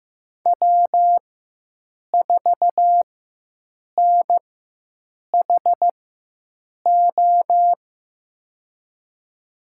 Morse code, heard as a keyed tone, about 15 wpm, spaced as for 7 wpm, 700 Hz. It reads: W4NHO
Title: USA Call Signs - 5 Characters - 4x Character Spacing - 15wpm